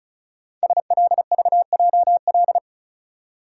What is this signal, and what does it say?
Morse code, heard as a keyed tone, about 36 wpm, 700 Hz. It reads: SLVJL
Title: Code Groups - 5 Letters - 35wpm